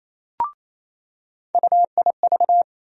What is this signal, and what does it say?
Morse code, heard as a keyed tone, about 28 wpm, 700 Hz. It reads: UIV
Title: USA Call Sign Postfixes - 3 Letters 28wpm